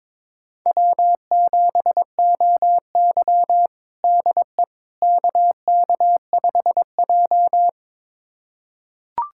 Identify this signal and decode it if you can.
Morse code, heard as a keyed tone, about 22 wpm, 700 Hz. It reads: W7OY DE KK5J
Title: USA Callsign DE Callsign - 4 Characters - 22wpm